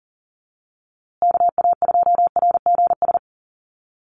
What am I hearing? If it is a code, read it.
Morse code, heard as a keyed tone, about 40 wpm, 700 Hz. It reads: KA2RGS
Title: USA Call Signs Repeated 3x 40wpm